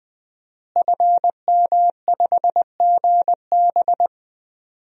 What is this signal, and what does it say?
Morse code, heard as a keyed tone, about 20 wpm, 700 Hz. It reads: FM5GB
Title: DXCC-100 Call Signs - 5 Characters - Repeated 3x - 20wpm